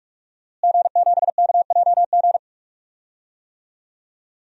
Morse code, heard as a keyed tone, about 41 wpm, 700 Hz. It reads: G7KJG
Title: DXCC-100 Call Signs - 5 Characters - Repeated 3x - 45wpm